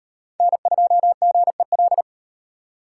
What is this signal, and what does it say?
Morse code, heard as a keyed tone, about 38 wpm, 700 Hz. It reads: N2GEL